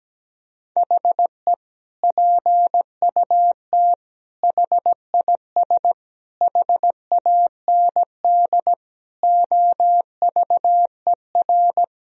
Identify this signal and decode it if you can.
Morse code, heard as a keyed tone, about 17 wpm, 700 Hz. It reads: HE PUT HIS HAND OVER
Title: Sentences from Top 1000 Words - Review Part 2 - 17wpm